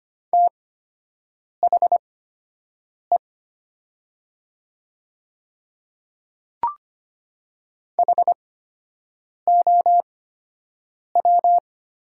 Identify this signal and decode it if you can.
Morse code, heard as a keyed tone, about 25 wpm, spaced as for 7 wpm, 700 Hz. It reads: THE HOW
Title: 8x Character Spacing - Top 100 Words - 25wpm